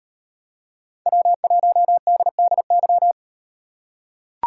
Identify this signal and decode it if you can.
Morse code, heard as a keyed tone, about 38 wpm, 700 Hz. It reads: W1DDY